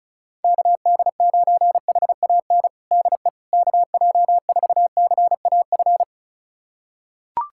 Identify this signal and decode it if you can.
Morse code, heard as a keyed tone, about 36 wpm, 700 Hz. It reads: KD9HAN DE KJ4CAF
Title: USA Callsign DE Callsign 35wpm